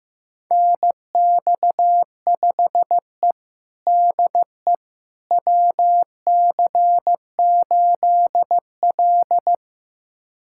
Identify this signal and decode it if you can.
Morse code, heard as a keyed tone, about 15 wpm, 700 Hz. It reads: NX5E DE WC8L
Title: USA Callsign DE Callsign - 4 Characters - 15wpm